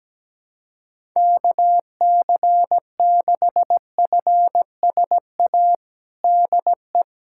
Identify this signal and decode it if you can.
Morse code, heard as a keyed tone, about 17 wpm, 700 Hz. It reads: KC6FSA DE
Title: USA Callsign DE Callsign 17wpm